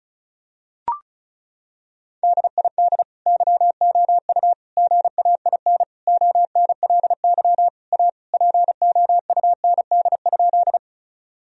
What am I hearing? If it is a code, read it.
Morse code, heard as a keyed tone, about 34 wpm, 700 Hz. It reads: DID YOU GAIN ONLY A POUND?
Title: Sentences from Top 1000 Words - Review Part 2 - 35wpm